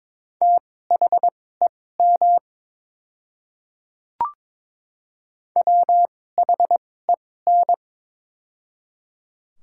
Morse code, heard as a keyed tone, about 23 wpm, spaced as for 16 wpm, 690 Hz. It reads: THEM WHEN
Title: 2x Character Spacing - Top 100 Words - 22wpm